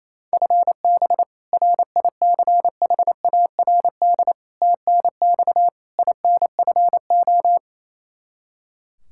Morse code, heard as a keyed tone, about 28 wpm, 700 Hz. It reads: FB RICHARD TNX INFO
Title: Practice QSO - Exchange 3 - More Info - 28wpm